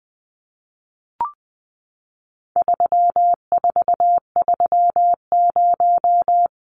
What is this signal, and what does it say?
Morse code, heard as a keyed tone, about 20 wpm, 700 Hz. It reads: 3430
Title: Numbers - 4-Digit 20wpm